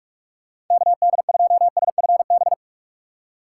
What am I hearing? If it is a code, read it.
Morse code, heard as a keyed tone, about 47 wpm, 700 Hz. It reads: KD2SFB
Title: USA Call Signs - 6 Characters - 3x Character Spacing - 45wpm